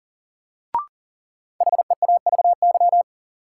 Morse code, heard as a keyed tone, about 40 wpm, 700 Hz. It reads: HEAVY
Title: Sentences from Top 1100 Words - New Word Drill Alt - 40wpm